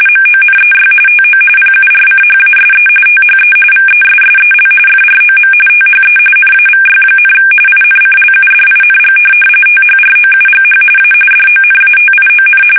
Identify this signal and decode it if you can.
RTTY, about 45.45 baud. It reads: MONEY TO PURCHASE THE VESSEL FROM THE NAVY, INCLUDING THROUGH A DONATION
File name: fsk850 / USS-Massachusetts